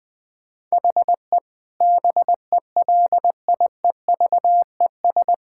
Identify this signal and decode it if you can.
Morse code, heard as a keyed tone, about 21 wpm, 700 Hz. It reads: HE BELIEVES
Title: Sentences with 9-Word Phrases - 20wpm